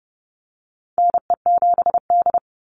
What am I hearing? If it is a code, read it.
Morse code, heard as a keyed tone, about 30 wpm, 700 Hz. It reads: NE7D
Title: Call Signs - 30wpm